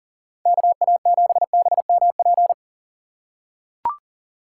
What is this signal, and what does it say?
Morse code, heard as a keyed tone, about 40 wpm, 700 Hz. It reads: KA7BMP
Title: USA Call Signs - 6 Characters - 4x Character Spacing - 40wpm